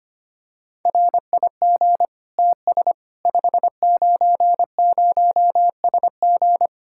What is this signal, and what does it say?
Morse code, heard as a keyed tone, about 25 wpm, 700 Hz. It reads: RIG TS 590SG
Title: K1USN SST QSO Practice 25wpm